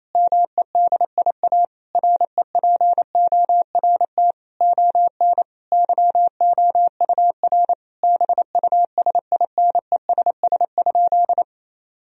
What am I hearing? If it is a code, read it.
Morse code, heard as a keyed tone, about 28 wpm, 700 Hz. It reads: MEDIA REPORT ON YOUR BUSINESS?
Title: Sentences from Top 1000 Words - Review Part 4 - 28wpm